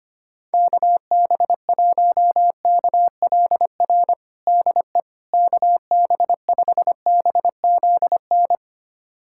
Morse code, heard as a keyed tone, about 25 wpm, 700 Hz. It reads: KB1KLR DE KB5BZN
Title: USA Callsign DE Callsign - 6 Characters - 25wpm